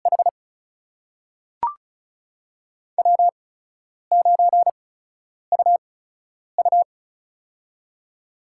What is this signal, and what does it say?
Morse code, heard as a keyed tone, about 36 wpm, spaced as for 10 wpm, 700 Hz. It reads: H W9UU
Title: USA Call Signs - 4 Characters - 8x Character Spacing - 35wpm